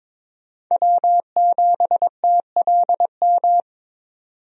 Morse code, heard as a keyed tone, about 22 wpm, 700 Hz. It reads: W7TLM
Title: USA Call Signs - 5 Characters - 22wpm